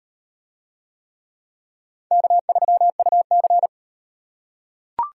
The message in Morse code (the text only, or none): K3UC